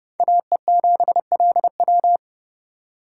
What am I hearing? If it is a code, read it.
Morse code, heard as a keyed tone, about 37 wpm, 700 Hz. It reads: AE7LW
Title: Speed Racing - USA 2x2 Call Signs 30wpm